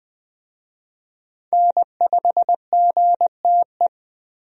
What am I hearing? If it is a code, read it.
Morse code, heard as a keyed tone, about 20 wpm, 700 Hz. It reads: N5GTE
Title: USA Call Signs - 5 Characters - Repeated 3x - 20wpm